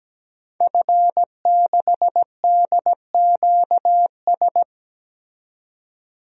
Morse code, heard as a keyed tone, about 17 wpm, 700 Hz. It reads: F6DQS
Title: DXCC-100 Call Signs - 5 Characters - 17wpm